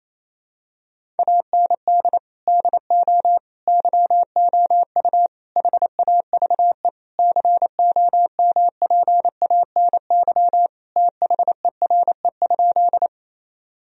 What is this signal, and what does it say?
Morse code, heard as a keyed tone, about 28 wpm, 700 Hz. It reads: AND DO YOU HAVE COMPANY THERE?
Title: Sentences from Top 1000 Words - Review Part 3 - 28wpm